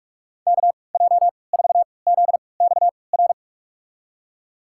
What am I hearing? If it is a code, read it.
Morse code, heard as a keyed tone, about 46 wpm, spaced as for 25 wpm, 700 Hz. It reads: KJ4ZXR
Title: USA Call Signs - 6 Characters - 3x Character Spacing - 45wpm